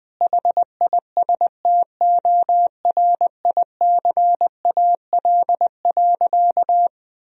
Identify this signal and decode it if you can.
Morse code, heard as a keyed tone, about 20 wpm, 700 Hz. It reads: HISTORICAL.
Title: Sentences from Top 1000 Words - Review Part 4 - 20wpm